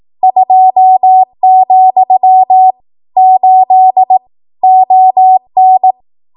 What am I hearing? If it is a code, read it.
Morse code, heard as a keyed tone, about 18 wpm, 750 Hz. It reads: 2, 8 ON
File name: Archive / 18 WPM / 18WPM